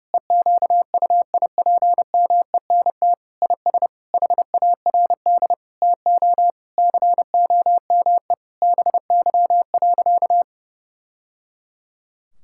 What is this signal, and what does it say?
Morse code, heard as a keyed tone, about 30 wpm, 700 Hz. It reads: EQUIPMENT IS HARD TO COME BY.
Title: Sentences from Top 1000 Words - Review Part 5 - 30wpm